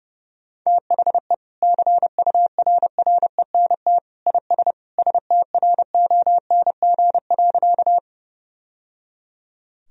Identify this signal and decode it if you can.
Morse code, heard as a keyed tone, about 32 wpm, 700 Hz. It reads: THE CURRENT IS STRONG.
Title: Sentences from Top 1000 Words - Review Part 4 - 30wpm